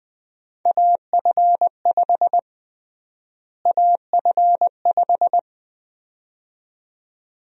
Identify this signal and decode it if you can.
Morse code, heard as a keyed tone, about 20 wpm, 700 Hz. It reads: AF5 AF5
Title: USA Call Sign Prefixes - Repeated 2x - 20wpm